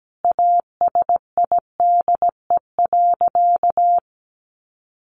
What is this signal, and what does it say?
Morse code, heard as a keyed tone, about 17 wpm, 700 Hz. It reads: ASIDE.
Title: A Sound Of Thunder 17wpm